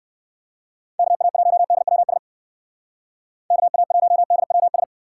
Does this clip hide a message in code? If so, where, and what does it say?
Yes: Morse code from 1.0 to 4.8 s, heard as a keyed tone, about 59 wpm, 700 Hz. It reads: KA1DWS KA1DWS